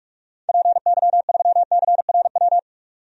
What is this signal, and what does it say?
Morse code, heard as a keyed tone, about 46 wpm, 700 Hz. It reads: PY3CRW
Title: DXCC-100 Call Signs - 6 Characters - 45wpm